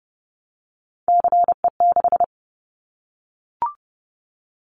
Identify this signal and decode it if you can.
Morse code, heard as a keyed tone, about 30 wpm, 700 Hz. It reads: CE6
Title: Callsign Prefixes for Top 100 DXCC - 30wpm